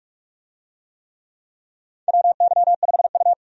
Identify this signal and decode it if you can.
Morse code, heard as a keyed tone, about 45 wpm, 700 Hz. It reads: WY5U
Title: USA Call Signs - 4 Characters - 45wpm